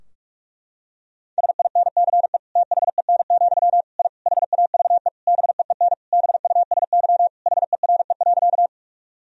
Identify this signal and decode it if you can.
Morse code, heard as a keyed tone, about 41 wpm, 700 Hz. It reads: SINCE THEN, I HAVE BEEN BUSY HERE.
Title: Sentences from Top 1000 Words - Review Part 3 - 45wpm